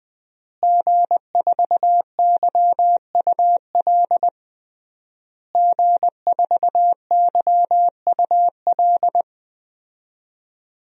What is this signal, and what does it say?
Morse code, heard as a keyed tone, about 20 wpm, 700 Hz. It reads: G4YUL G4YUL
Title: DXCC-100 Call Signs - 5 Characters - Repeated 3x - 20wpm